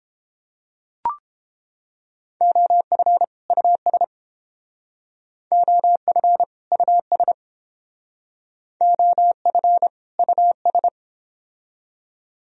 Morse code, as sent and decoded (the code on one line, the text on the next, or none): --- ..-. / ..- ... / --- ..-. / ..- ... / --- ..-. / ..- ...
OF US OF US OF US